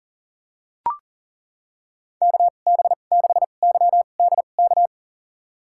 Morse code, heard as a keyed tone, about 38 wpm, spaced as for 28 wpm, 700 Hz. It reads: KB6YDK
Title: USA Call Signs - 6 Characters - 2x Character Spacing - 40wpm